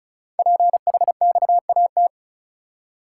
Morse code, heard as a keyed tone, about 36 wpm, 700 Hz. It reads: PHXAT
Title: Code Groups - 5 Letter:Numbers - 35wpm